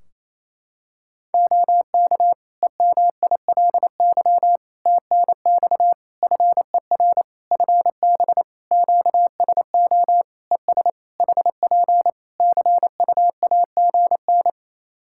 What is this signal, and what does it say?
Morse code, heard as a keyed tone, about 27 wpm, 700 Hz. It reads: OK EMILY TNX FER FB QSO ES HP CUAGN
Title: Practice QSO - Exchange 4 - Ending - 28wpm